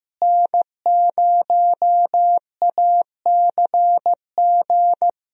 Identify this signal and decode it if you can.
Morse code, heard as a keyed tone, about 15 wpm, 700 Hz. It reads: N0ACG
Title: USA Call Signs - 5 Characters - Repeated 3x - 15wpm